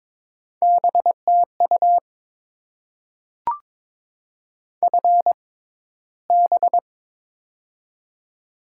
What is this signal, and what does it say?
Morse code, heard as a keyed tone, about 22 wpm, spaced as for 14 wpm, 690 Hz. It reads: BTU FB
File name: Top 20 QSO Elements - 6x Character Spacing - 22wpm